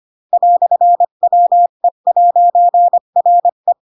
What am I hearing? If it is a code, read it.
Morse code, heard as a keyed tone, about 25 wpm, 700 Hz. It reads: "WE'RE
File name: Pinocchio / Pinocchio ch3 20 25 ik1hge